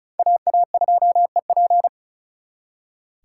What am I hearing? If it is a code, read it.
Morse code, heard as a keyed tone, about 44 wpm, 700 Hz. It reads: AA2EP